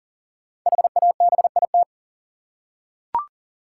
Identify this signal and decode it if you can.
Morse code, heard as a keyed tone, about 38 wpm, 700 Hz. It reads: HABIT
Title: Five Letter Words - 40wpm